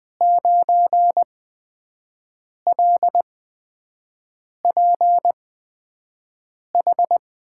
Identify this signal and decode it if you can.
Morse code, heard as a keyed tone, about 20 wpm, spaced as for 5 wpm, 700 Hz. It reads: 9LPH
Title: USA Call Signs - 6 Characters - 8x Character Spacing - 20wpm